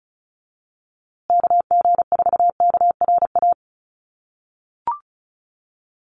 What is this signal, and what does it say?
Morse code, heard as a keyed tone, about 34 wpm, 700 Hz. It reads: KG4KRA
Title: USA Call Signs Repeated 2x 35wpm